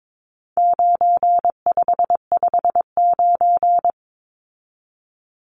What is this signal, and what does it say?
Morse code, heard as a keyed tone, about 22 wpm, 700 Hz. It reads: 9559
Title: Numbers - 4-Digit 22wpm